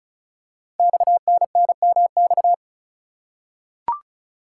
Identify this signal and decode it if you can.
Morse code, heard as a keyed tone, about 36 wpm, 690 Hz. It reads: XNNMX
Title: Code Groups - 5 Letters - 35wpm